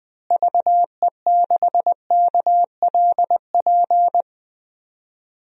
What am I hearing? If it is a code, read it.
Morse code, heard as a keyed tone, about 19 wpm, 700 Hz. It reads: VE6KLP
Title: DXCC-100 Call Signs - 6 Characters - Repeated 2x - 20wpm